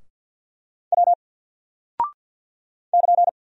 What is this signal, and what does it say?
Morse code, heard as a keyed tone, about 48 wpm, 700 Hz. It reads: W <KN>
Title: Single Character (Letter:Number:Punctuation:Prosign) 50wpm